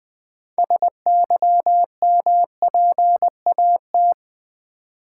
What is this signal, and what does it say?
Morse code, heard as a keyed tone, about 20 wpm, 700 Hz. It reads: SYMPAT
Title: Build A Word - 3 Letters At A Time - 8-Letter Words - 20wpm